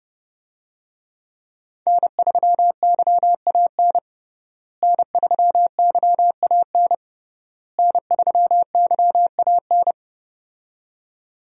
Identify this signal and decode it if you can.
Morse code, heard as a keyed tone, about 32 wpm, 700 Hz. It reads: N3YAN N3YAN N3YAN
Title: USA Call Signs - 5 Characters - Repeated 3x - 30wpm